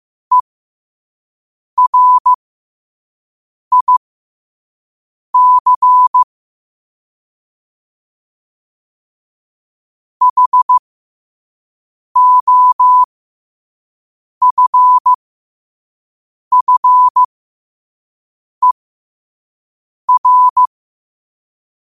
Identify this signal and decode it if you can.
Morse code, heard as a keyed tone, about 15 wpm, spaced as for 5 wpm, 1000 Hz. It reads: ERIC HOFFER
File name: qotdfiles / 5wpm